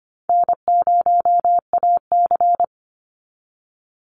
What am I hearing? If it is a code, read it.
Morse code, heard as a keyed tone, about 25 wpm, 700 Hz. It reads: N0AC
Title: Easy Call Signs - 25wpm